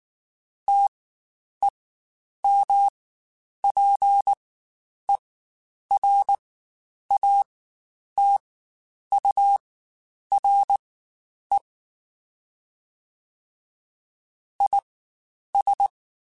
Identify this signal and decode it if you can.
Morse code, heard as a keyed tone, about 19 wpm, spaced as for 9 wpm, 800 Hz. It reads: TEMPERATURE IS
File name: Spaced Characters / Spaced Characters-mp3s / Weather 4 21 sp @18